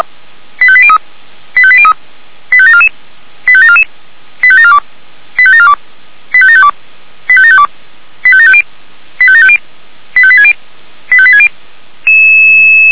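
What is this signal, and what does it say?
Five-tone selective call (ZVEI), heard as quick runs of five tones, tones of about 70 ms. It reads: 85803 85803 85744 85744 85742 85742 85752 85752 85750 85750 85860 85860
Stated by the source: ZVEI five-tone selective call, code 85725, received with an SDR.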